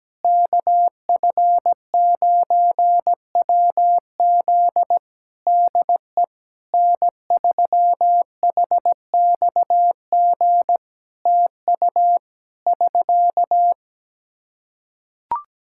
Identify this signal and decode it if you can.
Morse code, heard as a keyed tone, about 17 wpm, 700 Hz. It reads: KF9WZ DE N3HXG TU <SK>
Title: Practice QSO - Exchange 4 - Ending - 17wpm